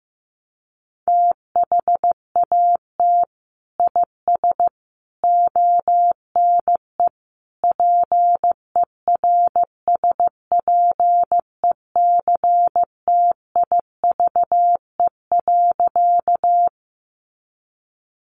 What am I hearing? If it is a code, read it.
Morse code, heard as a keyed tone, about 15 wpm, 700 Hz. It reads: THAT IS ONE PERSPECTIVE.
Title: Sentences from Top 1200 Words 15wpm